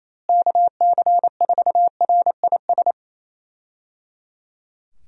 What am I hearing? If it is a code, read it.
Morse code, heard as a keyed tone, about 27 wpm, 700 Hz. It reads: KC4RIS
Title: USA Call Signs - 6 Characters - Repeated 3x - 28wpm